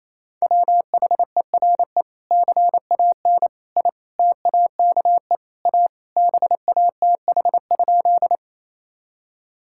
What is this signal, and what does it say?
Morse code, heard as a keyed tone, about 28 wpm, 700 Hz. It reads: WHERE CAN I TAKE A BATH?
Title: Sentences from Top 1600 Words 28wpm